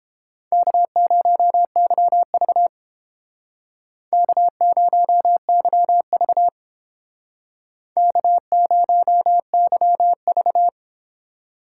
Morse code, heard as a keyed tone, about 31 wpm, 700 Hz. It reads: K0YV K0YV K0YV